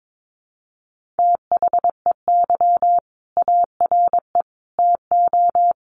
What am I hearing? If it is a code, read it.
Morse code, heard as a keyed tone, about 22 wpm, 700 Hz. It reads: THEY ARE TO
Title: Sentences from Top 1400 Words 22wpm